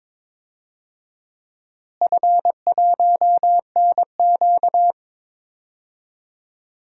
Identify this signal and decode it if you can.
Morse code, heard as a keyed tone, about 22 wpm, 700 Hz. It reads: F1NQ
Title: DXCC-100 Call Signs - 4 Characters - Repeated 2x - 22wpm